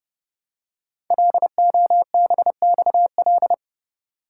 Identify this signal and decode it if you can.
Morse code, heard as a keyed tone, about 30 wpm, 700 Hz. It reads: LOBXL
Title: Code Groups - 5 Letters - 30wpm